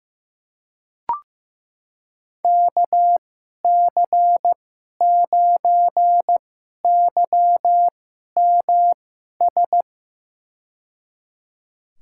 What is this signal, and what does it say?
Morse code, heard as a keyed tone, about 15 wpm, spaced as for 11 wpm, 700 Hz. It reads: KC9YMS
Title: USA Call Signs - 2x Character Spacing - 15wpm